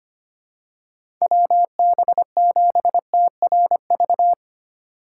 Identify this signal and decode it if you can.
Morse code, heard as a keyed tone, about 25 wpm, 700 Hz. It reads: WB7TRV
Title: USA Call Signs - 6 Characters - 4x Character Spacing - 25wpm